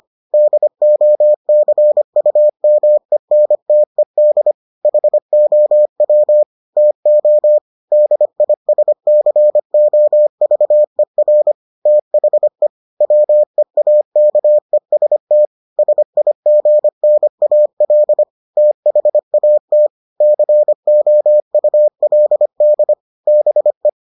Morse code, heard as a keyed tone, about 25 wpm, 600 Hz. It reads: DOCUMENTED HOW TO DISCOVER THE WEAKEST SIGNAL THAT COULD BE